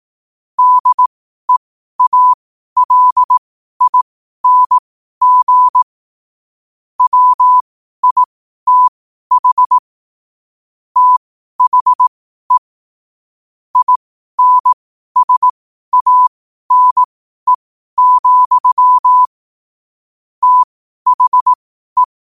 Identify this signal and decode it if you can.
Morse code, heard as a keyed tone, about 18 wpm, spaced as for 12 wpm, 1000 Hz. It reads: DEALING WITH THE INSANE, THE
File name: qotdfiles / 12wpm